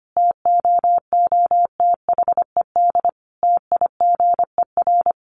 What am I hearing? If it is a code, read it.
Morse code, heard as a keyed tone, about 25 wpm, 700 Hz. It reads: TOOTHED TIGER
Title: A Sound Of Thunder 25wpm